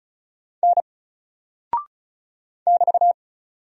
Morse code, heard as a keyed tone, about 34 wpm, 700 Hz. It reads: N <BT>
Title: Single Character (Letter:Number:Punctuation:Prosign) Alt 35wpm